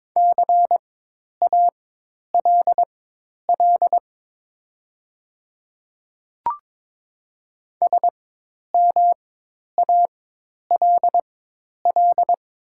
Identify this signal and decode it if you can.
Morse code, heard as a keyed tone, about 22 wpm, spaced as for 10 wpm, 700 Hz. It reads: CALL SMALL
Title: 4x Character Spacing - Top 200 Words - 22wpm